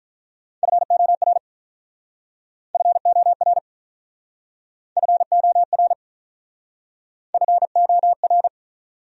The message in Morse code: FOR FOR FOR FOR

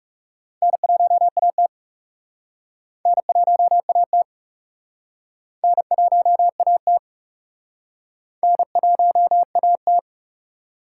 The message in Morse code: N1AT N1AT N1AT N1AT

